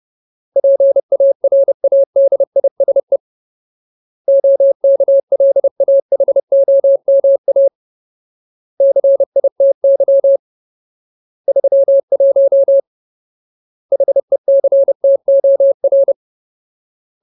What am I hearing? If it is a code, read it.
Morse code, heard as a keyed tone, about 30 wpm, 550 Hz. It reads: PARADISE OKLAHOMA CITY 31 HECTOR
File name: Ultimate Word List / 30WPM Ultimate